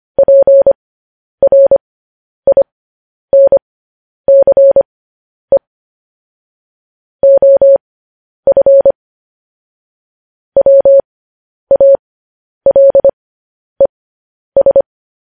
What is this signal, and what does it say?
Morse code, heard as a keyed tone, about 25 wpm, spaced as for 10 wpm, 550 Hz. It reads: PRINCE OF WALES
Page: news headlines in morse code at 10 wpm